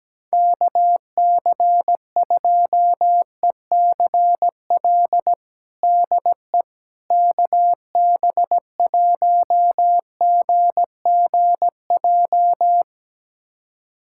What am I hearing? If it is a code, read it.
Morse code, heard as a keyed tone, about 17 wpm, 700 Hz. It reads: KC2ECL DE KB1GGJ
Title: USA Callsign DE Callsign - 6 Characters - 17wpm